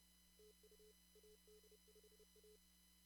Morse code, heard as a keyed tone, about 31 wpm, 440 Hz, with noise. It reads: TU AD5A
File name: CW / kt5j wpx cw-op-k5pi